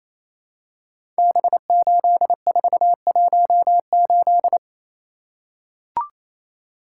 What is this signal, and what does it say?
Morse code, heard as a keyed tone, about 27 wpm, 700 Hz. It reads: B8418
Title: Code Groups - 5 Letter:Numbers - 28wpm